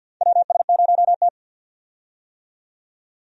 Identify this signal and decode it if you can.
Morse code, heard as a keyed tone, about 52 wpm, 700 Hz. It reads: WS0T
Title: USA Call Signs - 4 Characters - 50wpm